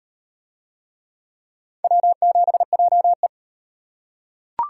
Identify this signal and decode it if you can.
Morse code, heard as a keyed tone, about 35 wpm, 700 Hz. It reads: W7JE